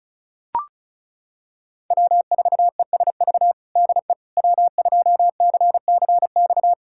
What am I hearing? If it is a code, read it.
Morse code, heard as a keyed tone, about 36 wpm, 700 Hz. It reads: W4ESV DE W2CCX
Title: USA Callsign DE Callsign - 5 Characters - 35wpm